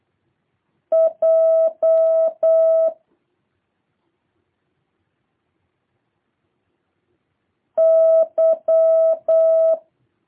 Morse code, heard as a keyed tone, about 8 wpm, 650 Hz. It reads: J Y